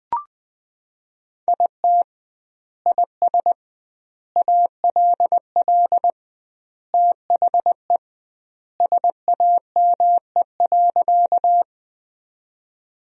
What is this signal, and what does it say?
Morse code, heard as a keyed tone, about 20 wpm, 700 Hz. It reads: IT IS ALL THE SAME.
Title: 2x Word-Spacing - Sentences from Top 200 Words - 20wpm